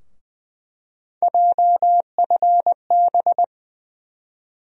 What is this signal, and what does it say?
Morse code, heard as a keyed tone, about 20 wpm, 700 Hz. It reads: JFB
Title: USA Call Sign Postfixes - 3 Letters 20wpm